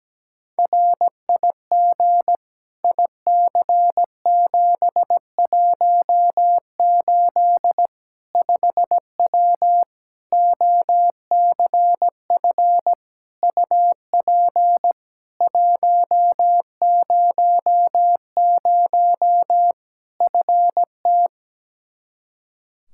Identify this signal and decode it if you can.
Morse code, heard as a keyed tone, about 17 wpm, 700 Hz. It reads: RIG IC718 5W OCF UP 100 FT
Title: K1USN SST QSO Practice 17wpm